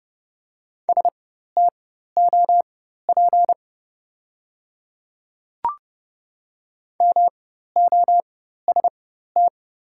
Morse code, heard as a keyed tone, about 30 wpm, spaced as for 14 wpm, 700 Hz. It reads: STOP MOST
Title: 4x Character Spacing - Top 500 Words - 30wpm